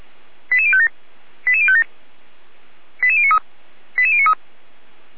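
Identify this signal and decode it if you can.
Five-tone selective call (ZVEI), heard as quick runs of five tones, tones of about 70 ms. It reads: 88057 88057 88093 88093
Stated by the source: ZVEI five-tone call received with an SDR, from fire brigade Feuerwehr Orlamünde.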